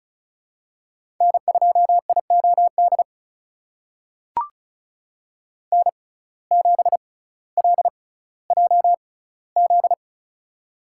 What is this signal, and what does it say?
Morse code, heard as a keyed tone, about 34 wpm, spaced as for 18 wpm, 700 Hz. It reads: N2IOD N7LJZ
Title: USA Call Signs - 5 Characters - 6x Character Spacing - 35wpm